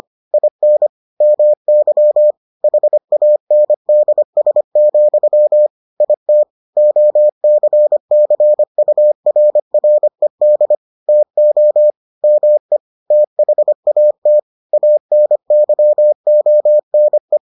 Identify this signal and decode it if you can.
Morse code, heard as a keyed tone, about 25 wpm, 600 Hz. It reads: IN MY HANDS, IT OCCURRED TO ME THAT ANYONE